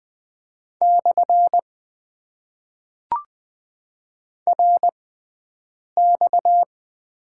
Morse code, heard as a keyed tone, about 20 wpm, spaced as for 7 wpm, 700 Hz. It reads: / RX
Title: Top 90 QSO Elements - 6x Character Spacing - 20wpm